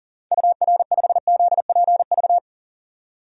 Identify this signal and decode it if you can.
Morse code, heard as a keyed tone, about 40 wpm, 700 Hz. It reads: UR5ZPV
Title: DXCC-100 Call Signs - 6 Characters - 40wpm